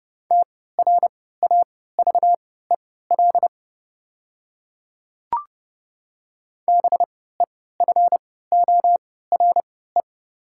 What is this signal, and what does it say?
Morse code, heard as a keyed tone, about 32 wpm, spaced as for 17 wpm, 700 Hz. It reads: TRAVEL BEFORE
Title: 3x Character Spacing - Top 500 Words - 30wpm